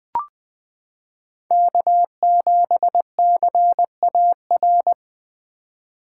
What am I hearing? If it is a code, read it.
Morse code, heard as a keyed tone, about 20 wpm, 700 Hz. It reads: K7CAR
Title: 2023 POTA Activator Call Signs - 20wpm